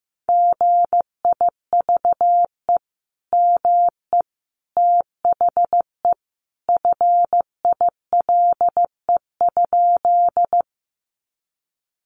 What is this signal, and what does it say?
Morse code, heard as a keyed tone, about 15 wpm, 700 Hz. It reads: GIVE ME THE FILE?
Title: Sentences from Top 1200 Words 15wpm